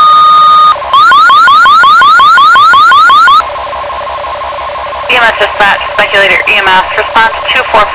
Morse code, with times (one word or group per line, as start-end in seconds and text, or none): none